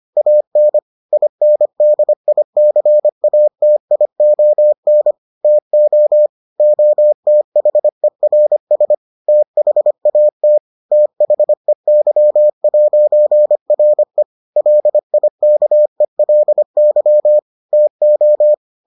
Morse code, heard as a keyed tone, about 25 wpm, 600 Hz. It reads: AN INDICATION TO OTHERS THAT THEY'RE LIKELY TO